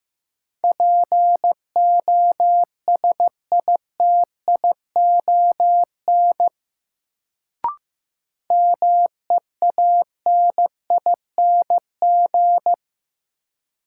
Morse code, heard as a keyed tone, about 15 wpm, 700 Hz. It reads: POSITION MEANING
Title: Sentences from Top 900 Words - New Word Drill - 15wpm